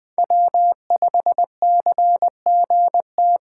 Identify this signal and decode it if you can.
Morse code, heard as a keyed tone, about 20 wpm, 700 Hz. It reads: W5CGT
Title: USA Call Signs - 5 Characters - 3x Character Spacing - 20wpm